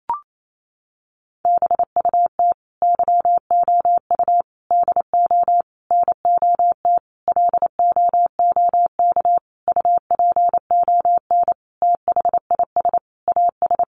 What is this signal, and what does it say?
Morse code, heard as a keyed tone, about 28 wpm, 700 Hz. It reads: BUT YOU DO NOT LOOK UPON THIS AS
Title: Sentences from Top 1100 Words 28wpm